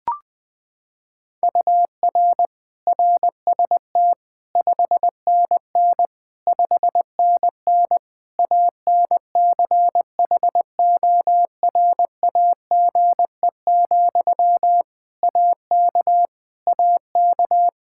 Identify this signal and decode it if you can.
Morse code, heard as a keyed tone, about 20 wpm, 700 Hz. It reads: UR RST 5NN 5NN ANCHORAGE, AK AK
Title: Practice QSO - Exchange 1 - Beginning - 20wpm